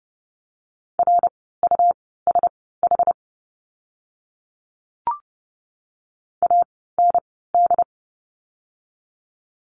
Morse code, heard as a keyed tone, about 30 wpm, spaced as for 17 wpm, 700 Hz. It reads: RUSH AND
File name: Lesson 17f - Letter u - Words 30wpm